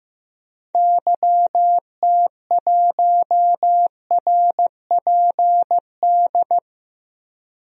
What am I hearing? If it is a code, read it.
Morse code, heard as a keyed tone, about 15 wpm, 700 Hz. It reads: YT1RPD
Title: DXCC-100 Call Signs - 6 Characters - 15wpm